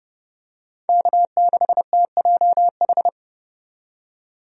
Morse code, heard as a keyed tone, about 30 wpm, 700 Hz. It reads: K6TJH
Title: USA Call Signs - 5 Characters - 30wpm